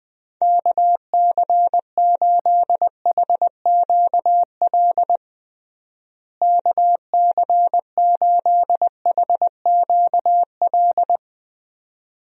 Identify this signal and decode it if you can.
Morse code, heard as a keyed tone, about 20 wpm, 700 Hz. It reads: KC8HQL KC8HQL
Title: USA Call Signs - 6 Characters - Repeated 2x - 20wpm